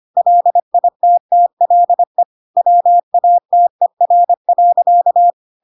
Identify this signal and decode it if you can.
Morse code, heard as a keyed tone, about 25 wpm, 700 Hz. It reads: LITTLE WATER.
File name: Pinocchio / Pinocchio ch5 25 25 ik1hge